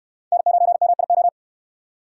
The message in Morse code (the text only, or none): N9MIO